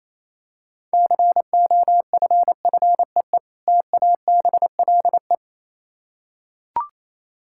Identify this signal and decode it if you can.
Morse code, heard as a keyed tone, about 27 wpm, 700 Hz. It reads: COFFEE TABLE
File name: Easy Words - Speed-Racing with Hints - 22wpm